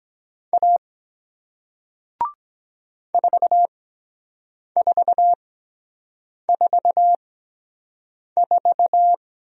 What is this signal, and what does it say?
Morse code, heard as a keyed tone, about 23 wpm, spaced as for 7 wpm, 700 Hz. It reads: A 4444